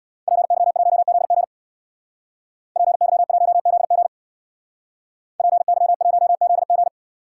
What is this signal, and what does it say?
Morse code, heard as a keyed tone, about 64 wpm, 700 Hz. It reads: PY1ZG PY1ZG PY1ZG